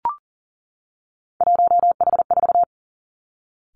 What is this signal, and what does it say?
Morse code, heard as a keyed tone, about 39 wpm, 700 Hz. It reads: 1H4
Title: Three Challenging Letter-Number Combinations 40wpm